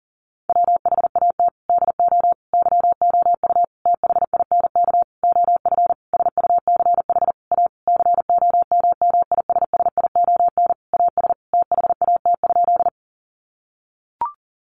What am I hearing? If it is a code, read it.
Morse code, heard as a keyed tone, about 40 wpm, 700 Hz. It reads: WHAT DO YOU THINK OF SUCH A COMMISSION AS THAT?
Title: Sentences from Top 1400 Words 40wpm